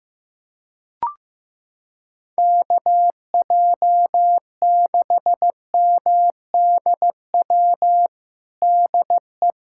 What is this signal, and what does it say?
Morse code, heard as a keyed tone, about 16 wpm, 700 Hz. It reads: KJ6MDW DE
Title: USA Callsign DE Callsign 15wpm